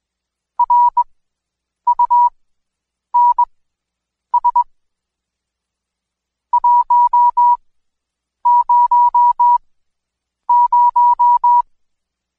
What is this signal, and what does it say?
Morse code, heard as a keyed tone, about 20 wpm, spaced as for 8 wpm, 970 Hz. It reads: RUNS 100
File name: Code Quick / AudioDownld2 / Seg114-Day19-DayDrill 19